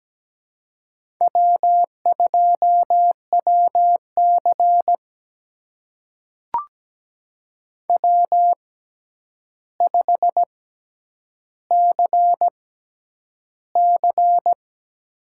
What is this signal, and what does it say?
Morse code, heard as a keyed tone, about 17 wpm, spaced as for 9 wpm, 700 Hz. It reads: W2WC W5CC
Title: USA Call Signs - 4 Characters - 6x Character Spacing - 17wpm